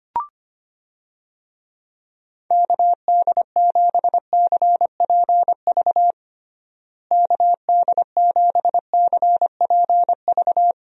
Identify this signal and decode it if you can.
Morse code, heard as a keyed tone, about 25 wpm, 700 Hz. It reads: KD7CPV KD7CPV
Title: USA Call Signs - 6 Characters - Repeated 2x - 25wpm